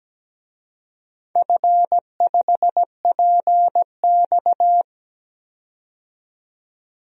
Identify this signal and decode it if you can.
Morse code, heard as a keyed tone, about 17 wpm, 700 Hz. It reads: F5PX
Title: DXCC-100 Call Signs - 4 Characters - 17wpm